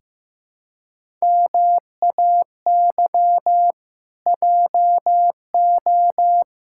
Morse code, heard as a keyed tone, about 16 wpm, 700 Hz. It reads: MAY JO
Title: Pangram Challenge v2 - 15wpm